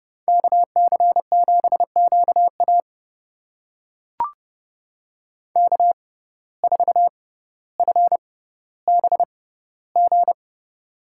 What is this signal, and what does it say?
Morse code, heard as a keyed tone, about 32 wpm, spaced as for 16 wpm, 700 Hz. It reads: KC7QA K4FBG
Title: USA Call Signs - 5 Characters - 6x Character Spacing - 30wpm